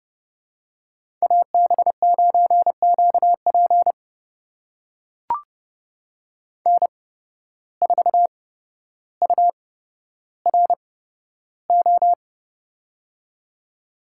Morse code, heard as a keyed tone, about 30 wpm, spaced as for 13 wpm, 700 Hz. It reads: AB9QP N4URO
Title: USA Call Signs - 5 Characters - 8x Character Spacing - 30wpm